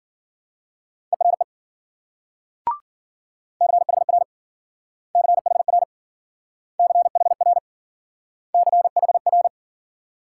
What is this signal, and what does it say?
Morse code, heard as a keyed tone, about 50 wpm, 700 Hz. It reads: ERE CHR CHR CHR CHR
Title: Speed-Racing Multi-letter Phonemes - 40wpm